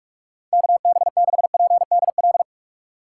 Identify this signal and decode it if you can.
Morse code, heard as a keyed tone, about 44 wpm, 700 Hz. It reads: KB6PDL